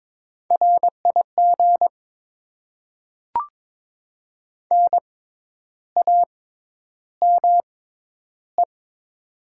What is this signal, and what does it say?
Morse code, heard as a keyed tone, about 22 wpm, spaced as for 10 wpm, 700 Hz. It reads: RIG NAME
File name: Top 20 QSO Elements - 6x Character Spacing - 22wpm